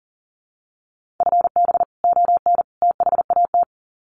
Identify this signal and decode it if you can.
Morse code, heard as a keyed tone, about 37 wpm, 700 Hz. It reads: FB ON THAT
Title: Common QSO Phrases 40wpm